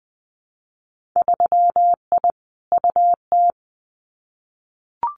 Morse code, heard as a keyed tone, about 19 wpm, 700 Hz. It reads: 3I UT
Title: Winter Field Day Exchanges Encore 20wpm